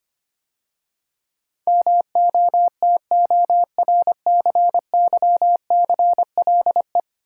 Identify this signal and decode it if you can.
Morse code, heard as a keyed tone, about 25 wpm, 700 Hz. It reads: MOTORCYCLE
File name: 10 Letter Words - First 3 Letters Given - 25wpm